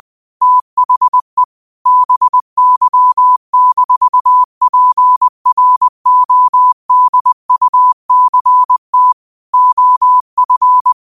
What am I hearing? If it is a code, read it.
Morse code, heard as a keyed tone, about 20 wpm, 1000 Hz. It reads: THE BY-PRODUCT OF